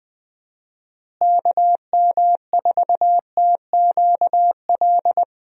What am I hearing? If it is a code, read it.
Morse code, heard as a keyed tone, about 21 wpm, 700 Hz. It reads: KM4TQL
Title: USA Call Signs - 6 Characters - Repeated 3x - 20wpm